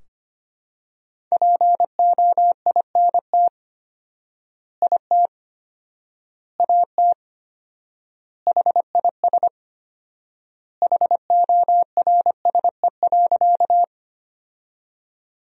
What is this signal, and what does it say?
Morse code, heard as a keyed tone, about 25 wpm, 700 Hz. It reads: POINT IT AT HIS HORSE.
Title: 4x Word-Spacing - Sentences from Top 300 Words - 25wpm